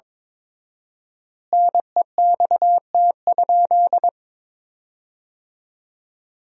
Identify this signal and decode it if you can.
Morse code, heard as a keyed tone, about 21 wpm, 700 Hz. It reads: NEXT?
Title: 4x Word-Spacing - Sentences from Top 300 Words - 22wpm